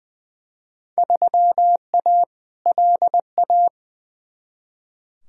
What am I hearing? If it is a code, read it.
Morse code, heard as a keyed tone, about 20 wpm, 700 Hz. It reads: 3A LA
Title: Field Day Exchanges - 20wpm